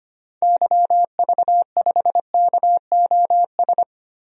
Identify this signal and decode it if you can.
Morse code, heard as a keyed tone, about 25 wpm, 700 Hz. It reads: YV5KOS
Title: DXCC-100 Call Signs - 6 Characters - Repeated 2x - 25wpm